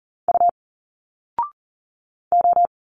Morse code, heard as a keyed tone, about 40 wpm, 700 Hz. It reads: U O
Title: Single Letters 40wpm